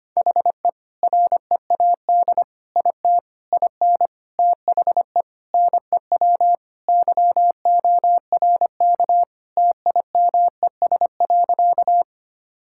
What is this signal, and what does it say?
Morse code, heard as a keyed tone, about 25 wpm, 700 Hz. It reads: HE READ IT IN THE NEW YORK TIMES.
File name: practice / 25wpm / Sentences with 4-Word Phrases - 25wpm